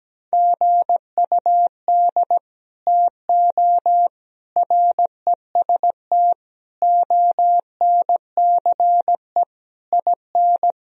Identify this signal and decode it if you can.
Morse code, heard as a keyed tone, about 17 wpm, 700 Hz. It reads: GUD TO REST ONCE IN
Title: Sentences from Top 500 Words and Abbreviations 17wpm